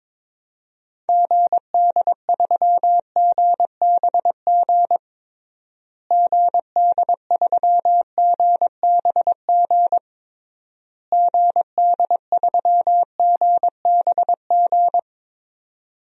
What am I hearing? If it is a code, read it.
Morse code, heard as a keyed tone, about 22 wpm, 700 Hz. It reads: GD3GBG GD3GBG GD3GBG
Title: DXCC-100 Call Signs - 6 Characters - Repeated 3x - 22wpm